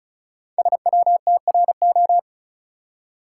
Morse code, heard as a keyed tone, about 36 wpm, 700 Hz. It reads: SWTRO
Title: Code Groups - 5 Letters - 35wpm